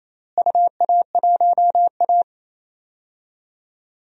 Morse code, heard as a keyed tone, about 28 wpm, 700 Hz. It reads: UA1A
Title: DXCC-100 Call Signs - 4 Characters - Repeated 3x - 28wpm